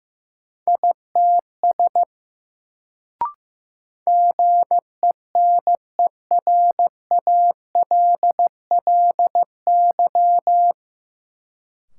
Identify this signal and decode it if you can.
Morse code, heard as a keyed tone, about 15 wpm, 700 Hz. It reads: ITS GENERALLY
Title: Sentences from Top 600 Words - New Word Drill - 15wpm